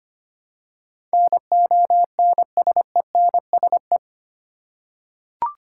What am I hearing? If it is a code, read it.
Morse code, heard as a keyed tone, about 25 wpm, 690 Hz. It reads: NONSENSE
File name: 8 Letter Words - First 3 Letters Given - 25wpm